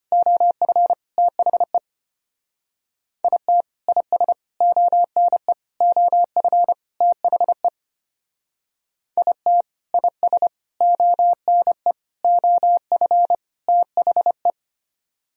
Morse code, heard as a keyed tone, about 29 wpm, 700 Hz. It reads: OF THE IT IS ONE OF THE IT IS ONE OF THE
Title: Speed-Racing - Sets of 5 Words 25wpm